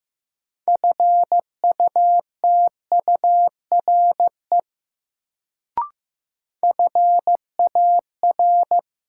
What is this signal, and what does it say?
Morse code, heard as a keyed tone, about 15 wpm, 700 Hz. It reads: FUTURE FAR
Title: Sets of 3 Words - New Word Drill - 15wpm